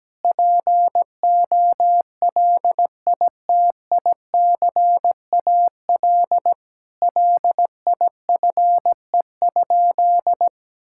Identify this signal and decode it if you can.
Morse code, heard as a keyed tone, about 17 wpm, 700 Hz. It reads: POLITICAL LIFE?
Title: Sentences from Top 1000 Words - Review Part 5 - 17wpm